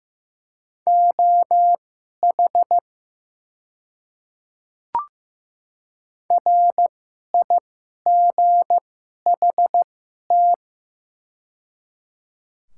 Morse code, heard as a keyed tone, about 15 wpm, spaced as for 11 wpm, 700 Hz. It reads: OH RIGHT
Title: 2x Character Spacing - Top 500 Words - 15wpm